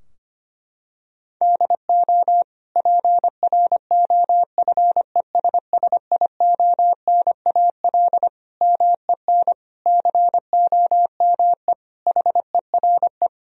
Morse code, heard as a keyed tone, about 25 wpm, 700 Hz. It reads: DO PROFESSIONAL MEN COME HERE
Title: Sentences from Top 1000 Words - Review Part 4 - 25wpm